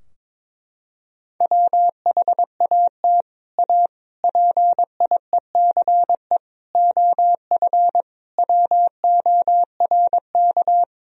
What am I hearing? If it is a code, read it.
Morse code, heard as a keyed tone, about 22 wpm, 700 Hz. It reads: WHAT A PIECE OF WORK
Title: Sentences from Top 1000 Words - Review Part 2 - 22wpm